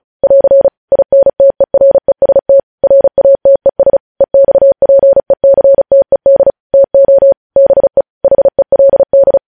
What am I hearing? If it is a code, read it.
Morse code, heard as a keyed tone, about 36 wpm, 550 Hz. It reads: <AR> INTEREST RATES EXPECTED TO BE HELD
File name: podcast / News headlines-1x-35x35